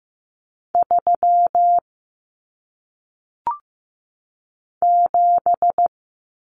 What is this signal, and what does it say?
Morse code, heard as a keyed tone, about 15 wpm, 700 Hz. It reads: 3 7
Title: Numbers - 1-Digit 15wpm